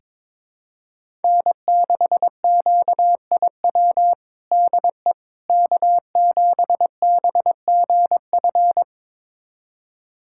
Morse code, heard as a keyed tone, about 22 wpm, 700 Hz. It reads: N6QIW DE K7BGF
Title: USA Callsign DE Callsign - 5 Characters - 22wpm